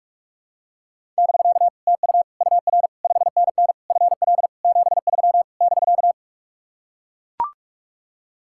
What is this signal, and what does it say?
Morse code, heard as a keyed tone, about 44 wpm, 700 Hz. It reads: <BK> TU UR 5NN FL 73 <BK>